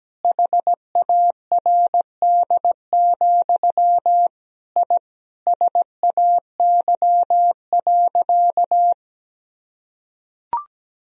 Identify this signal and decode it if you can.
Morse code, heard as a keyed tone, about 17 wpm, 700 Hz. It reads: HARD, I SAY.
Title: Sentences from Top 1000 Words - Review Part 2 - 17wpm